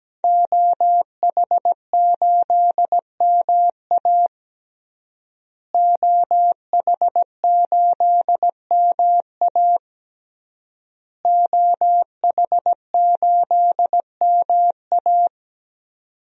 Morse code, heard as a keyed tone, about 17 wpm, 700 Hz. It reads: OH8MA OH8MA OH8MA
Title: DXCC-100 Call Signs - 5 Characters - Repeated 3x - 17wpm